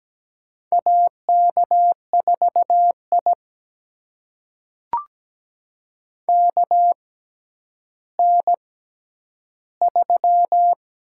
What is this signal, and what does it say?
Morse code, heard as a keyed tone, about 17 wpm, spaced as for 10 wpm, 700 Hz. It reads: AK4I KN3
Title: USA Call Signs - 4 Characters - 6x Character Spacing - 17wpm